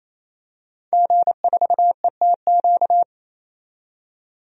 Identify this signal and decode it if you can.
Morse code, heard as a keyed tone, about 28 wpm, 700 Hz. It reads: G4ETQ
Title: DXCC-100 Call Signs - 5 Characters - 28wpm